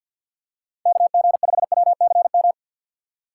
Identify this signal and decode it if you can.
Morse code, heard as a keyed tone, about 49 wpm, 700 Hz. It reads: KG5WCM
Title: USA Call Signs - 6 Characters - 4x Character Spacing - 50wpm